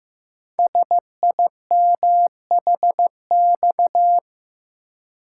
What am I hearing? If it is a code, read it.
Morse code, heard as a keyed tone, about 15 wpm, 700 Hz. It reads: SIMHX
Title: Code Groups - 5 Letters - 15wpm